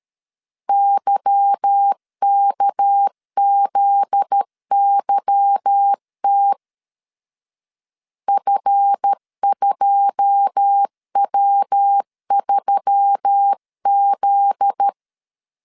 Morse code, heard as a keyed tone, about 13 wpm, 800 Hz. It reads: YKZYT F2W3Z
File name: CW / DET / 1 9 1500C S B 12wpm 3819